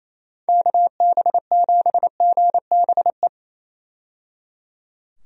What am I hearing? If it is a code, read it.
Morse code, heard as a keyed tone, about 28 wpm, 700 Hz. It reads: KB7GBE
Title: USA Call Signs - 6 Characters - 28wpm